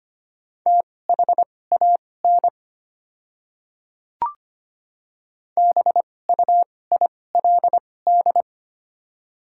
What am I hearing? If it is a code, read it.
Morse code, heard as a keyed tone, about 25 wpm, spaced as for 18 wpm, 690 Hz. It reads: THAN BUILD
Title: 2x Character Spacing - Top 200 Words - 25wpm